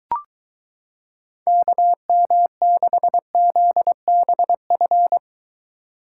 Morse code, heard as a keyed tone, about 23 wpm, 700 Hz. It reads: KM6ZBF